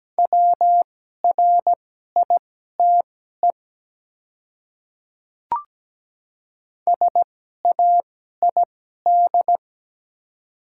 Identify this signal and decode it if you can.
Morse code, heard as a keyed tone, about 17 wpm, spaced as for 12 wpm, 700 Hz. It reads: WRITE SAID
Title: 2x Character Spacing - Top 100 Words - 17wpm